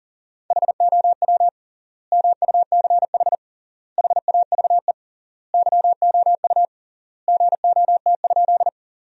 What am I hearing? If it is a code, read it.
Morse code, heard as a keyed tone, about 40 wpm, 700 Hz. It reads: HOW MUCH HAVE YOU GOT?
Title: 3x Word-Spacing - Sentences from Top 300 Words - 40wpm